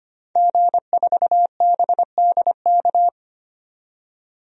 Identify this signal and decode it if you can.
Morse code, heard as a keyed tone, about 25 wpm, 700 Hz. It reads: G4BDK
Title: DXCC-100 Call Signs - 5 Characters - 25wpm